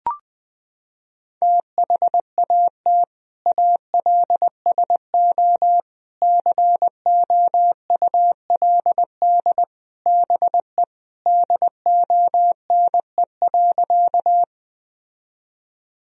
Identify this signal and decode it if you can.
Morse code, heard as a keyed tone, about 20 wpm, 700 Hz. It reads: THAT ALSO COULD BE DONE.
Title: Sentences from Top 1000 Words - Review Part 1 - 20wpm